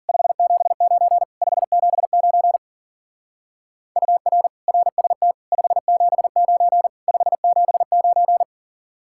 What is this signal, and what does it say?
Morse code, heard as a keyed tone, about 51 wpm, 700 Hz. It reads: 579 579 UR RST 579 579